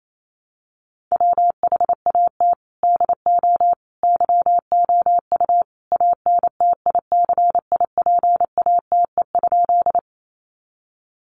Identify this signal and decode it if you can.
Morse code, heard as a keyed tone, about 28 wpm, 700 Hz. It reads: WHAT DO YOU ANTICIPATE?
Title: Sentences from Top 1500 Words 28wpm